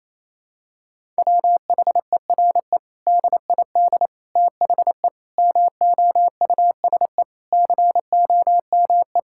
Morse code, heard as a keyed tone, about 27 wpm, 700 Hz. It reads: WHERE DID THE MOUSE COME
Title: Sentences from Top 1000 Words - Review Part 5 - 28wpm